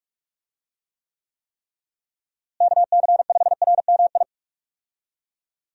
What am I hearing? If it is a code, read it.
Morse code, heard as a keyed tone, about 46 wpm, 700 Hz. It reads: KC5RMI
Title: USA Call Signs - 6 Characters - Repeated 3x - 45wpm